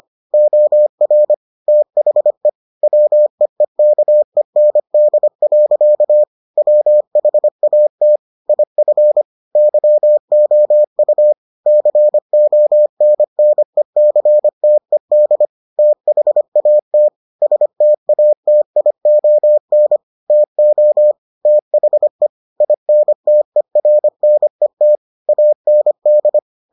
Morse code, heard as a keyed tone, about 25 wpm, 600 Hz. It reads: OR THE WEEKEND. WHAT IF YOU CONNECTED THAT STATION TO THE INTERNET AND